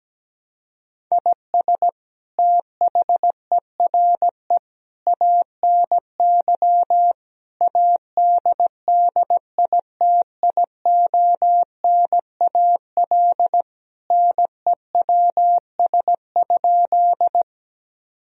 Morse code, heard as a keyed tone, about 17 wpm, 700 Hz. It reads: IS THERE ANY ADDITIONAL NEWS?
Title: Sentences from Top 1000 Words - Review Part 4 - 17wpm